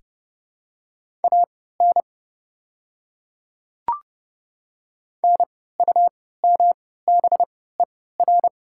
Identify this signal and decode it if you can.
Morse code, heard as a keyed tone, about 30 wpm, spaced as for 17 wpm, 700 Hz. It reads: AN NUMBER
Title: 3x Character Spacing - Top 100 Words - 30wpm